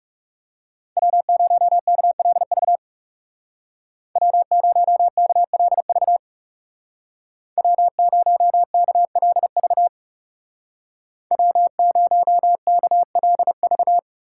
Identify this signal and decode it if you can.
Morse code, heard as a keyed tone, about 39 wpm, 700 Hz. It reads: W0KLV W0KLV W0KLV W0KLV